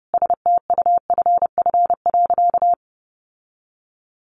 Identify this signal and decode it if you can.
Morse code, heard as a keyed tone, about 30 wpm, 700 Hz. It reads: STUFF.
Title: Sentences from Top 1400 Words 30wpm